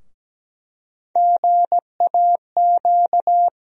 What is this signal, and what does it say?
Morse code, heard as a keyed tone, about 17 wpm, 700 Hz. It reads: GAQ
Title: USA Call Sign Postfixes - 3 Letters 17wpm